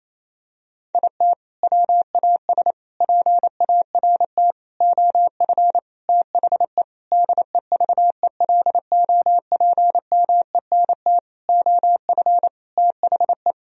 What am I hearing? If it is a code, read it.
Morse code, heard as a keyed tone, about 28 wpm, 700 Hz. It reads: IT WAS PART OF THE DEVELOPMENT OF THE
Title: Sentences from Top 1000 Words - Review Part 4 - 28wpm